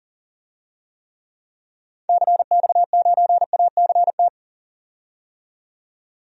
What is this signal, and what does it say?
Morse code, heard as a keyed tone, about 40 wpm, 700 Hz. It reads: CX9ACT
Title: DXCC-100 Call Signs - 6 Characters - 40wpm